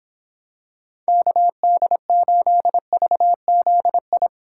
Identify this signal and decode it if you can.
Morse code, heard as a keyed tone, about 26 wpm, 700 Hz. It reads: KD8VZI